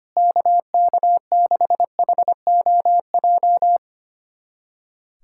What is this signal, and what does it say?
Morse code, heard as a keyed tone, about 25 wpm, 700 Hz. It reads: KK6HOJ
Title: USA Callsign DE Callsign - 6 Characters - 25wpm